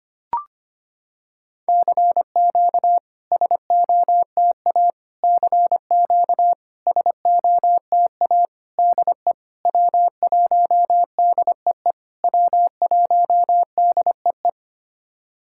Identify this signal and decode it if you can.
Morse code, heard as a keyed tone, about 25 wpm, 700 Hz. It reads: CQ SOTA CQ SOTA DE W1DEE W1DEE
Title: SOTA QSO Practice 25wpm